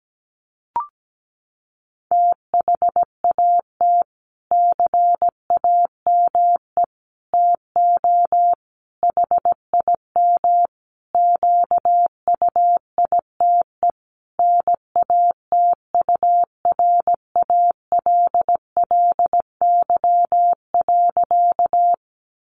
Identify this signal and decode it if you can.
Morse code, heard as a keyed tone, about 17 wpm, 700 Hz. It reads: THAT CAME TO HIM QUITE NATURALLY.
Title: Sentences from Top 1300 Words 17wpm